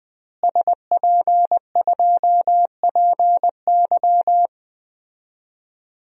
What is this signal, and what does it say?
Morse code, heard as a keyed tone, about 21 wpm, 700 Hz. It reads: SP2PY
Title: DXCC-100 Call Signs - 5 Characters - Repeated 2x - 20wpm